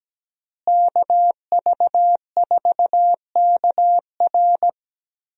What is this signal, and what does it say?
Morse code, heard as a keyed tone, about 17 wpm, 700 Hz. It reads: KV4KR
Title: USA Call Signs - 5 Characters - Repeated 3x - 17wpm